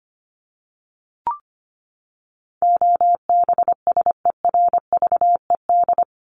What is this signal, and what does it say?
Morse code, heard as a keyed tone, about 25 wpm, 700 Hz. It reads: OBSERVED
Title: ED Suffix - Top 100 Words with Base Words 25wpm